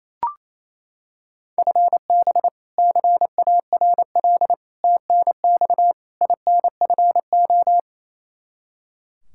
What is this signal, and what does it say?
Morse code, heard as a keyed tone, about 28 wpm, 700 Hz. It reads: FB CARL TNX INFO
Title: Practice QSO - Exchange 3 - More Info - 28wpm